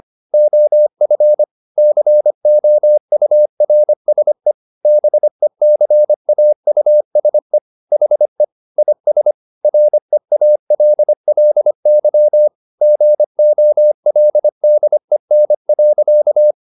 Morse code, heard as a keyed tone, about 25 wpm, 600 Hz. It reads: OF COURSE BECAUSE HE IS REALLY GOLDEN.